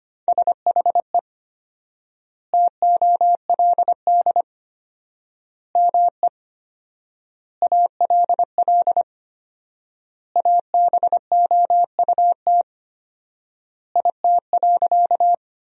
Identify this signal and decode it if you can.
Morse code, heard as a keyed tone, about 25 wpm, 700 Hz. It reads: SHE TOLD ME ALL ABOUT IT.
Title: 4x Word-Spacing - Sentences from Top 400 Words - 25wpm